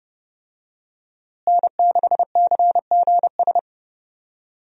Morse code, heard as a keyed tone, about 31 wpm, 700 Hz. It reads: N6CGS
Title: USA Call Signs - 5 Characters - Repeated 3x - 30wpm